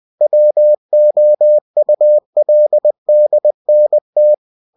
Morse code, heard as a keyed tone, about 20 wpm, 600 Hz. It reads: WOULDNT
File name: cw-books / winnie-the-pooh / winnie-20wpm-page-0041